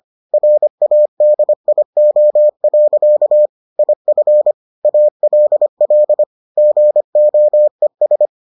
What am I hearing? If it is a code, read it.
Morse code, heard as a keyed tone, about 25 wpm, 600 Hz. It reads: RADIO. IF ALL GOES